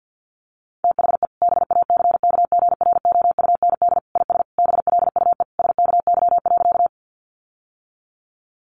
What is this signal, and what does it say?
Morse code, heard as a keyed tone, about 50 wpm, 700 Hz. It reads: THE BACKGROUND IS BLUE SKY.
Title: Sentences from Top 1300 Words 50wpm